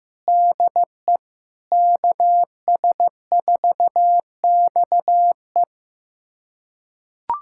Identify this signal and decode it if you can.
Morse code, heard as a keyed tone, about 15 wpm, 700 Hz. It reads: DE KS4XE
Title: USA Callsign DE Callsign 15wpm